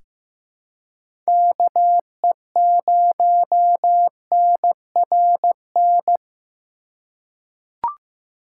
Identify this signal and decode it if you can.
Morse code, heard as a keyed tone, about 15 wpm, 700 Hz. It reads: KE0NRN
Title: USA Call Signs - 4x Character Spacing - 15wpm